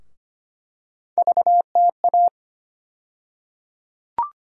Morse code, heard as a keyed tone, about 25 wpm, 700 Hz. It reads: VTA
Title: USA Call Sign Postfixes - 3 Letters 25wpm